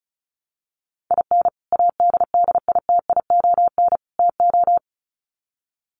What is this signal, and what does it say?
Morse code, heard as a keyed tone, about 34 wpm, 700 Hz. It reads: IN ADDITION TO
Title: Sets of 3 Words 35wpm